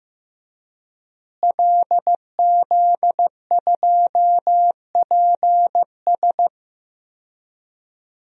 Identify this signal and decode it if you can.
Morse code, heard as a keyed tone, about 15 wpm, 700 Hz. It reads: LZ2PS
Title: DXCC-100 Call Signs - 5 Characters - Repeated 2x - 15wpm